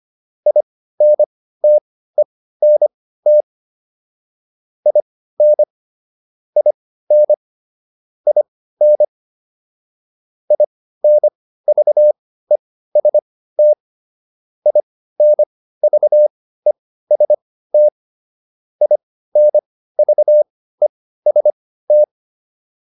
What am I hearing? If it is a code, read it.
Morse code, heard as a keyed tone, about 25 wpm, spaced as for 15 wpm, 600 Hz. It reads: INTENT IN IN IN INVEST INVEST INVEST